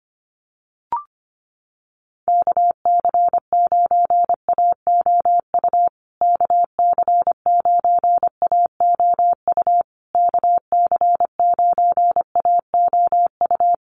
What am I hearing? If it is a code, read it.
Morse code, heard as a keyed tone, about 25 wpm, 700 Hz. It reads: KC9AOU KC9AOU KC9AOU
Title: USA Call Signs Repeated 3x 25wpm